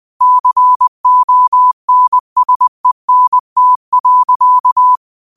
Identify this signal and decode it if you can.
Morse code, heard as a keyed tone, about 20 wpm, 1000 Hz. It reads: CONSENT.
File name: qotdfiles / 20wpm